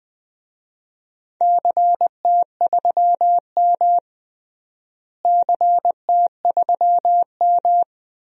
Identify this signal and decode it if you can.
Morse code, heard as a keyed tone, about 20 wpm, 700 Hz. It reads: CT3M CT3M
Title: DXCC-100 Call Signs - 4 Characters - Repeated 2x - 20wpm